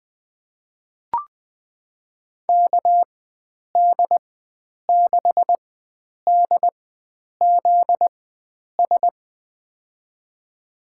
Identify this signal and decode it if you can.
Morse code, heard as a keyed tone, about 20 wpm, spaced as for 9 wpm, 700 Hz. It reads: KD6DZS
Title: USA Call Signs - 6 Characters - 4x Character Spacing - 20wpm